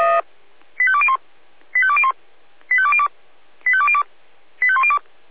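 Five-tone selective call (ZVEI), heard as quick runs of five tones, tones of about 70 ms. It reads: 85201 85201 85202 85202 85102
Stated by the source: ZVEI five-tone selective call received on an SDR.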